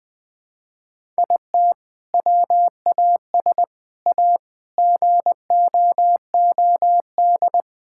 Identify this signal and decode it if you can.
Morse code, heard as a keyed tone, about 20 wpm, 700 Hz. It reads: IT WAS A GOOD
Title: Sentences from Top 1000 Words - Review Part 5 - 20wpm